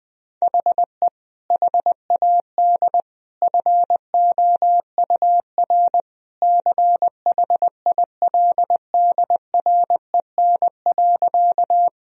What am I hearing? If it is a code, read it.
Morse code, heard as a keyed tone, about 21 wpm, 700 Hz. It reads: HE HAD FOUR CHILDREN.
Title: Sentences from Top 1000 Words - Review Part 1 - 20wpm